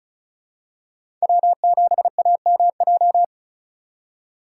Morse code, heard as a keyed tone, about 36 wpm, 700 Hz. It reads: W7AMJ
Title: USA Call Signs - 5 Characters - 2x Character Spacing - 35wpm